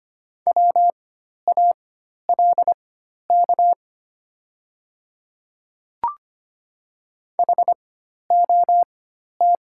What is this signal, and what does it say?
Morse code, heard as a keyed tone, about 25 wpm, spaced as for 12 wpm, 690 Hz. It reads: WALK HOT
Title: 4x Character Spacing - Top 300 Words - 25wpm